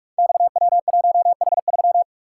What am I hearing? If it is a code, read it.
Morse code, heard as a keyed tone, about 44 wpm, 700 Hz. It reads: XW1H3
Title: Code Groups - 5 Letter:Numbers - 45wpm